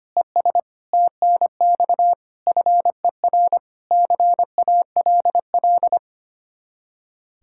Morse code, heard as a keyed tone, about 31 wpm, 700 Hz. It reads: ES TNX FER CALL
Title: Speed-Racing Common QSO Phrases Encore - 25wpm